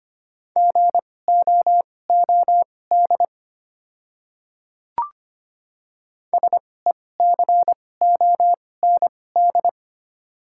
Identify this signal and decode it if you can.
Morse code, heard as a keyed tone, about 25 wpm, spaced as for 18 wpm, 690 Hz. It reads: GOOD SECOND
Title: 2x Character Spacing - Top 500 Words - 25wpm